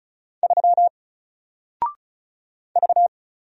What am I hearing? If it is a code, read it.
Morse code, heard as a keyed tone, about 36 wpm, 700 Hz. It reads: 3 V
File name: Single-Letter-Number-35wpm